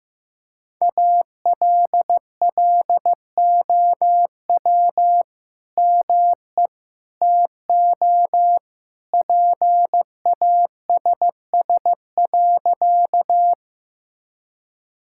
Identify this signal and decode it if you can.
Morse code, heard as a keyed tone, about 16 wpm, 700 Hz. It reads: ALLOW ME TO PASS.
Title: Sentences from Top 1000 Words - Review Part 5 - 15wpm